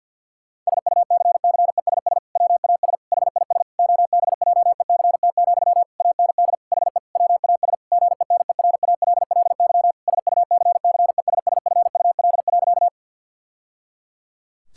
Morse code, heard as a keyed tone, about 50 wpm, 700 Hz. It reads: SUCCESS WAS HIS OBJECT, AND HE WAS GENERALLY SUCCESSFUL.
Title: Sentences from Top 1000 Words - Review Part 6 - 50wpm